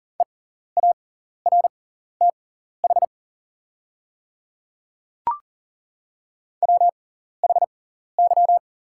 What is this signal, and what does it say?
Morse code, heard as a keyed tone, about 40 wpm, spaced as for 14 wpm, 700 Hz. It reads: EARTH WHY
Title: 6x Character Spacing - Top 500 Words - 40wpm